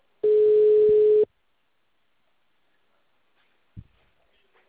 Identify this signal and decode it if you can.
Ringback tone (Europe, 425 Hz).